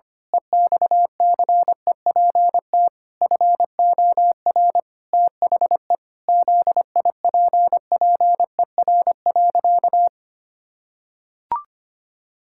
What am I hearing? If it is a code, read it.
Morse code, heard as a keyed tone, about 25 wpm, 700 Hz. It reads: EXCEPT FOR THE ZIPPER.
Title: Pangram Challenge v2 - 25wpm